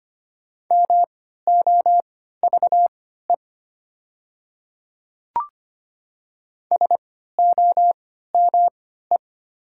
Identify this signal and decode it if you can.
Morse code, heard as a keyed tone, about 25 wpm, spaced as for 14 wpm, 700 Hz. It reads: MOVE SOME
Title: 3x Character Spacing - Top 500 Words - 25wpm